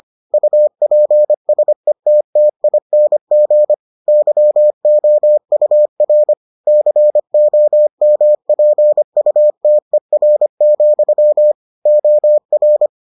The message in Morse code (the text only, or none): UPSETTING YOUR COMPUTER, OR